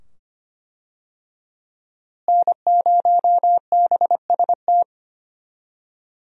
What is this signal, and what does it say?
Morse code, heard as a keyed tone, about 25 wpm, 700 Hz. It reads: N0BST
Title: USA Call Signs - 5 Characters - Repeated 2x - 25wpm